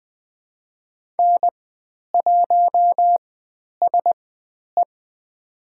Morse code, heard as a keyed tone, about 20 wpm, spaced as for 10 wpm, 700 Hz. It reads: N1SE
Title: 109 Lesson 18g - Letter m - Call Signs 20wpm